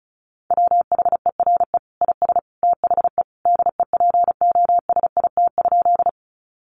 Morse code, heard as a keyed tone, about 34 wpm, 700 Hz. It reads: WHERE IS THE DEPOSIT?
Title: Sentences from Top 1300 Words 35wpm